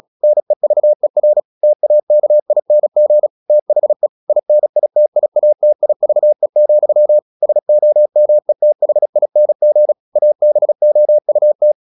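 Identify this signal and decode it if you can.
Morse code, heard as a keyed tone, about 36 wpm, 600 Hz. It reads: NEVER TAKING THE INITIATIVE, SOMETHING ABOUT
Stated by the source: Morse practice at 36 wpm, text of Ken Follett's novel Never, from CW med Gnister.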